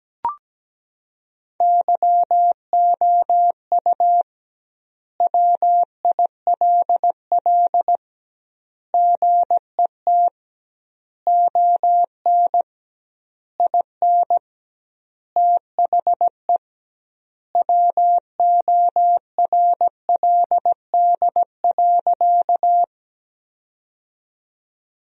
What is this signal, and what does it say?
Morse code, heard as a keyed tone, about 17 wpm, 700 Hz. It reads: YOU WILL GET ON IN THE WORLD.
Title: 2x Word-Spacing - Sentences from Top 200 Words - 17wpm